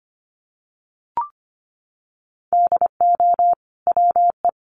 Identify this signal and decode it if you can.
Morse code, heard as a keyed tone, about 25 wpm, 700 Hz. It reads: DO WE
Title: Sentences from Top 300 Words 25wpm